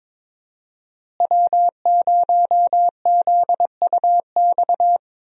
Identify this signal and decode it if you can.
Morse code, heard as a keyed tone, about 22 wpm, 700 Hz. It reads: W0ZUX
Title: USA Call Signs - 5 Characters - Repeated 2x - 22wpm